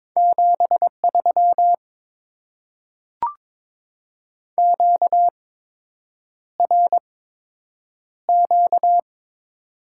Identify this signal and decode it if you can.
Morse code, heard as a keyed tone, about 22 wpm, spaced as for 8 wpm, 690 Hz. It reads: 73 QRQ
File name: Top 90 QSO Elements - 8x Character Spacing - 22wpm